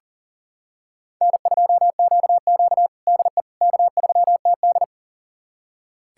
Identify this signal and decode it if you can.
Morse code, heard as a keyed tone, about 40 wpm, 700 Hz. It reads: N2QQ DE K3TD
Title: USA Callsign DE Callsign - 4 Characters - 40wpm